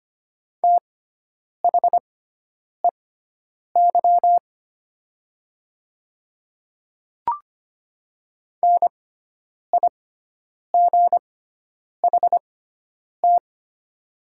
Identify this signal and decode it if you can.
Morse code, heard as a keyed tone, about 25 wpm, spaced as for 9 wpm, 700 Hz. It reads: THEY NIGHT
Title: 6x Character Spacing - Top 300 Words - 25wpm